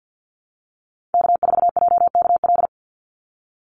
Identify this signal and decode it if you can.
Morse code, heard as a keyed tone, about 50 wpm, 700 Hz. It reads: K4JKL
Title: Call Signs Repeated 2x - 50wpm